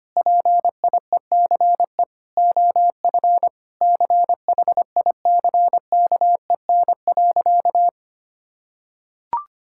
Morse code, heard as a keyed tone, about 25 wpm, 700 Hz. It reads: PIECE OF CHICKEN.
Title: Sentences from Top 1000 Words - Review Part 5 - 25wpm